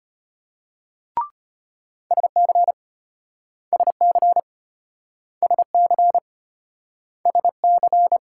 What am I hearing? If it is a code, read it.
Morse code, heard as a keyed tone, about 33 wpm, 700 Hz. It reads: SC SC SC SC